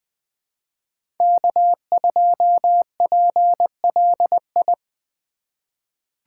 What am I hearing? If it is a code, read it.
Morse code, heard as a keyed tone, about 20 wpm, 700 Hz. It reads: K2PLI
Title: USA Call Signs - 5 Characters - 4x Character Spacing - 20wpm